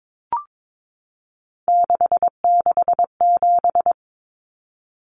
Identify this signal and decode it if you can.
Morse code, heard as a keyed tone, about 22 wpm, 700 Hz. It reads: 667
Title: Common Cut Numbers - 3-Digit 22wpm